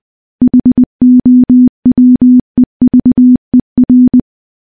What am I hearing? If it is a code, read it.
Morse code, heard as a keyed tone, about 20 wpm, 250 Hz. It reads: HOWEVER
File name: morse-training-20WPM